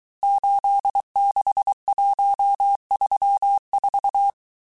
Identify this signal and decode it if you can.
Morse code, heard as a keyed tone, about 23 wpm, 800 Hz. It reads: 86134
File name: cours-du-mercredi / audios / cours-855-texte-chiffre-vitesse-22-mpm